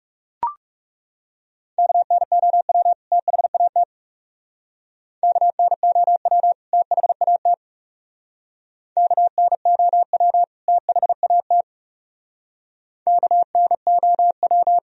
Morse code, heard as a keyed tone, about 37 wpm, 700 Hz. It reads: KNOW THAT KNOW THAT KNOW THAT KNOW
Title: Speed-Racing - Sets of 2 Words 30wpm